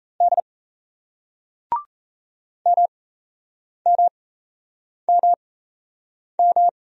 Morse code, heard as a keyed tone, about 40 wpm, spaced as for 8 wpm, 700 Hz. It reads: D MMMM